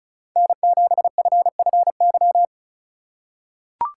Morse code, heard as a keyed tone, about 34 wpm, 700 Hz. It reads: N7FFY
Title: USA Call Signs - 2x Character Spacing - 35wpm